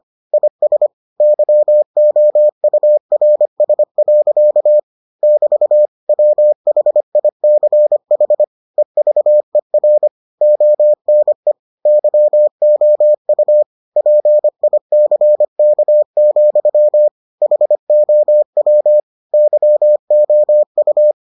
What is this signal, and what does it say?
Morse code, heard as a keyed tone, about 25 wpm, 600 Hz. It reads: IS YOURS. <BT> WHICH EVER ONE YOU PICK, HOW YOU